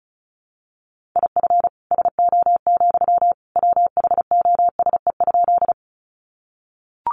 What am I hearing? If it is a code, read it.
Morse code, heard as a keyed tone, about 34 wpm, 700 Hz. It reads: IF SO, WHOSE?
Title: Sentences from Top 1200 Words 35wpm